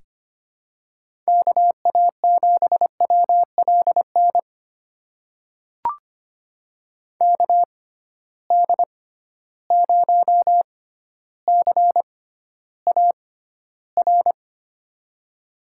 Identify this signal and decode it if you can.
Morse code, heard as a keyed tone, about 25 wpm, spaced as for 13 wpm, 700 Hz. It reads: KA7WLN KD0CAR
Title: USA Call Signs - 6x Character Spacing - 25wpm